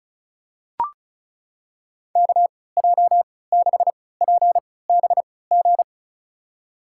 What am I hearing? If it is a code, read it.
Morse code, heard as a keyed tone, about 36 wpm, spaced as for 20 wpm, 700 Hz. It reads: KJ6PBG
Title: USA Call Signs - 3x Character Spacing - 35wpm